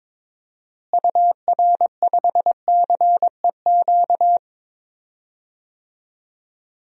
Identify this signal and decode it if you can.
Morse code, heard as a keyed tone, about 22 wpm, 700 Hz. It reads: UR5CEQ
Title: DXCC-100 Call Signs - 6 Characters - Repeated 2x - 22wpm